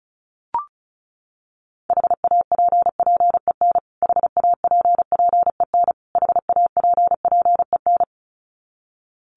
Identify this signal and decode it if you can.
Morse code, heard as a keyed tone, about 34 wpm, 700 Hz. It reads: HAPPEN HAPPEN HAPPEN
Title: Top 500 Words Repeated 35wpm